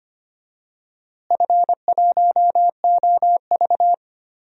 Morse code, heard as a keyed tone, about 25 wpm, 700 Hz. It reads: F1OV
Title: DXCC-100 Call Signs - 4 Characters - Repeated 3x - 25wpm